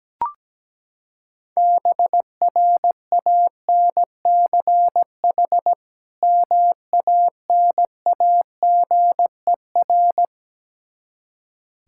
Morse code, heard as a keyed tone, about 17 wpm, 700 Hz. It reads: BRANCH MANAGER
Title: K1USN SST QSO Practice 17wpm